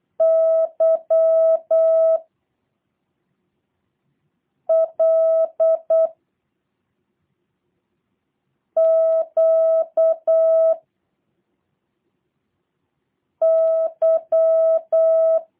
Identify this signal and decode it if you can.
Morse code, heard as a keyed tone, about 8 wpm, 650 Hz. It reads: Y L Q Y
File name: jh8cbh / 2023cw / review1-6